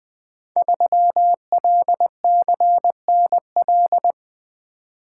Morse code, heard as a keyed tone, about 20 wpm, 700 Hz. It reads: 3LCNL
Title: Code Groups - 5 Letter:Numbers - 20wpm